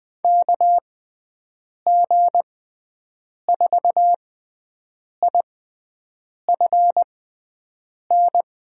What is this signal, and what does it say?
Morse code, heard as a keyed tone, about 20 wpm, spaced as for 7 wpm, 700 Hz. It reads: KG4IFN
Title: USA Call Signs - 6x Character Spacing - 20wpm